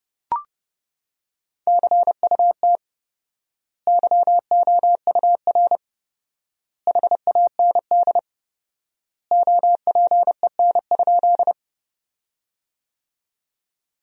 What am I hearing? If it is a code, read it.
Morse code, heard as a keyed tone, about 30 wpm, 700 Hz. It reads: CUT YOUR HAND OPEN?
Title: 4x Word-Spacing - Sentences from Top 300 Words - 30wpm